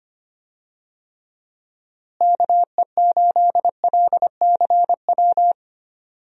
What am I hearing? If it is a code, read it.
Morse code, heard as a keyed tone, about 25 wpm, 700 Hz. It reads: KE8LCW
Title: USA Call Signs - 6 Characters - Repeated 2x - 25wpm